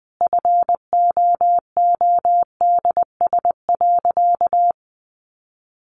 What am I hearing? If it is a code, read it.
Morse code, heard as a keyed tone, about 20 wpm, 700 Hz. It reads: FOODS.
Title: Easy Phrases 20wpm